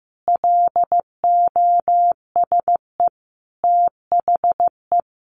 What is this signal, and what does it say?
Morse code, heard as a keyed tone, about 15 wpm, 700 Hz. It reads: LOSE THE
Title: Sentences from Top 1300 Words 15wpm